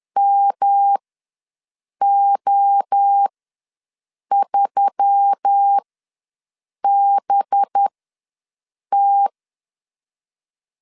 Morse code, heard as a keyed tone, about 10 wpm, spaced as for 6 wpm, 800 Hz. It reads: MO3BT
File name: LEZIONI / 7 GF9Z / 7 G ws 10wpm 3843